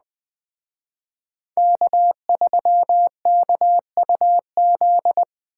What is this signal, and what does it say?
Morse code, heard as a keyed tone, about 20 wpm, 700 Hz. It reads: K3KUZ